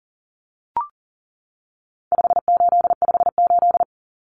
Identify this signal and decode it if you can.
Morse code, heard as a keyed tone, about 40 wpm, 700 Hz. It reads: 5858
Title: Numbers - 4-Digit 40wpm